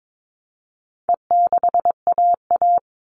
Morse code, heard as a keyed tone, about 22 wpm, 700 Hz. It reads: E6AA
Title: All Cut Numbers - 4-Digit 22wpm